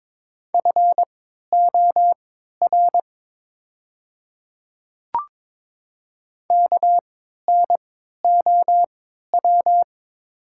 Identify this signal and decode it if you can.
Morse code, heard as a keyed tone, about 22 wpm, spaced as for 12 wpm, 700 Hz. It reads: FOR KNOW
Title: 3x Character Spacing - Top 100 Words - 22wpm